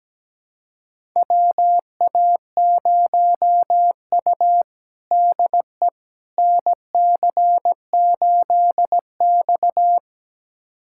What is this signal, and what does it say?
Morse code, heard as a keyed tone, about 17 wpm, 700 Hz. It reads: WA0U DE NC8X
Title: USA Callsign DE Callsign - 4 Characters - 17wpm